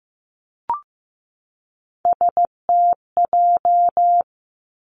Morse code, heard as a keyed tone, about 15 wpm, 700 Hz. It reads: STJ
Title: Three Challenging Letter-Number Combinations 15wpm